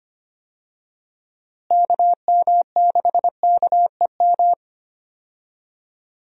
Morse code, heard as a keyed tone, about 25 wpm, 700 Hz. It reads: KM6KEM
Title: USA Call Signs - 6 Characters - Repeated 2x - 25wpm